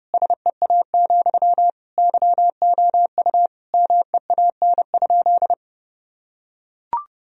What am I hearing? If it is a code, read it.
Morse code, heard as a keyed tone, about 30 wpm, 700 Hz. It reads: SEA, YOU MEAN?
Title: Sentences from Top 1000 Words - Review Part 1 - 30wpm